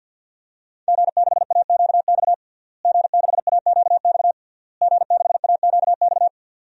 Morse code, heard as a keyed tone, about 53 wpm, 700 Hz. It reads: G6AQX G6AQX G6AQX
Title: DXCC-100 Call Signs - 5 Characters - Repeated 3x - 50wpm